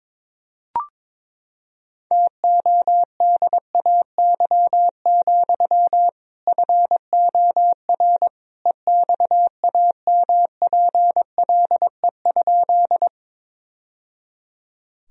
Morse code, heard as a keyed tone, about 22 wpm, 700 Hz. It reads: TODAY, FOR EXAMPLE?
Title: Sentences from Top 1000 Words - Review Part 1 - 22wpm